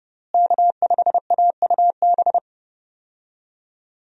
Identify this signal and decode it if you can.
Morse code, heard as a keyed tone, about 32 wpm, 700 Hz. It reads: K5AUB
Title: USA Call Signs - 5 Characters - Repeated 2x - 30wpm